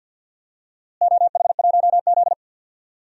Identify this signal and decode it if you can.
Morse code, heard as a keyed tone, about 51 wpm, 700 Hz. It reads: OH1Z
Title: DXCC-100 Call Signs - 4 Characters - Repeated 2x - 50wpm